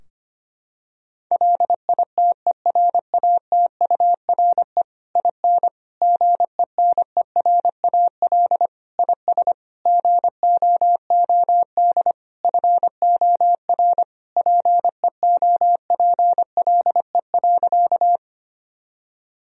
Morse code, heard as a keyed tone, about 25 wpm, 700 Hz. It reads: LITERATURE IN GENERAL IS GOOD FOR PEOPLE.
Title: Sentences from Top 1000 Words - Review Part 3 - 25wpm